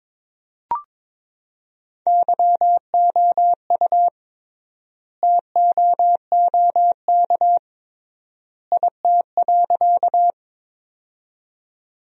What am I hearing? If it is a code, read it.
Morse code, heard as a keyed tone, about 22 wpm, 700 Hz. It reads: YOU TOOK IT.
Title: 3x Word-Spacing - Sentences from Top 300 Words - 22wpm